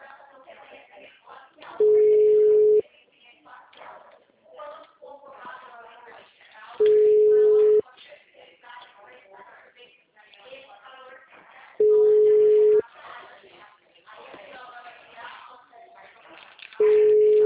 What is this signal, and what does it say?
Ringback tone (Europe, 425 Hz), with speech.
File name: out- -2004- - - .31169